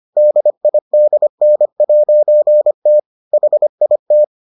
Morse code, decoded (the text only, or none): DIDN'T HIT